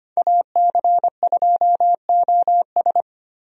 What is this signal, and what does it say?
Morse code, heard as a keyed tone, about 25 wpm, 700 Hz. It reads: AC2OS
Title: USA Call Signs - 5 Characters - Repeated 2x - 25wpm